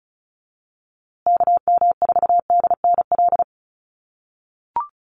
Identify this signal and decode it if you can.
Morse code, heard as a keyed tone, about 34 wpm, 700 Hz. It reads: KM4DNL
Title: USA Call Signs Repeated 3x 35wpm